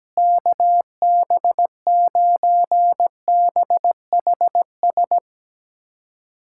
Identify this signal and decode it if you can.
Morse code, heard as a keyed tone, about 17 wpm, 700 Hz. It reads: KB9BHS
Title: USA Callsign DE Callsign 17wpm